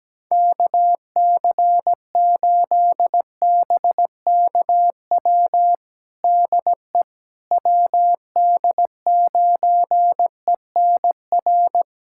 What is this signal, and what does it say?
Morse code, heard as a keyed tone, about 17 wpm, 700 Hz. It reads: KC8BKW DE WD9ENR
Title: USA Callsign DE Callsign - 6 Characters - 17wpm